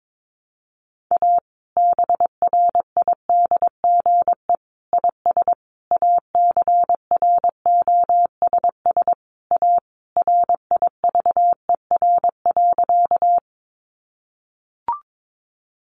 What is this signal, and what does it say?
Morse code, heard as a keyed tone, about 22 wpm, 700 Hz. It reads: A BRIDGE IS ACROSS A RIVER.
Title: Sentences from Top 1200 Words 22wpm